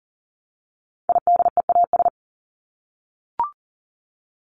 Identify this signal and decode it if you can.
Morse code, heard as a keyed tone, about 40 wpm, 700 Hz. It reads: IDEAS
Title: ES:S Suffix - Top 100 Words with Base Words 40wpm